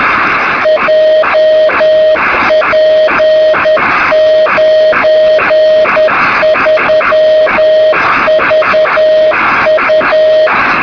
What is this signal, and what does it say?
Morse code, heard as a keyed tone, about 10 wpm, 620 Hz, with noise. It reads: JP93VU
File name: sm7lcb / 2203z sk2shf-b 23cm